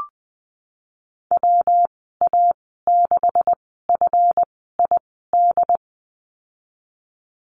Morse code, heard as a keyed tone, about 20 wpm, spaced as for 15 wpm, 700 Hz. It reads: WA6FID
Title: 173 Lesson 28g - Number 9 - Call Signs 20wpm